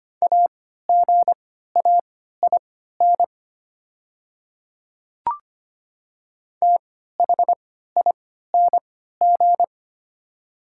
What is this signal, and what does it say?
Morse code, heard as a keyed tone, about 25 wpm, spaced as for 14 wpm, 690 Hz. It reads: AGAIN THING
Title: 3x Character Spacing - Top 200 Words - 25wpm